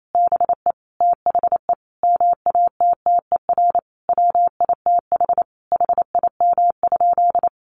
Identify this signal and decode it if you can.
Morse code, heard as a keyed tone, about 28 wpm, 700 Hz. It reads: BE THE MATTER WITH HIM?
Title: Sentences from Top 700 Words 28wpm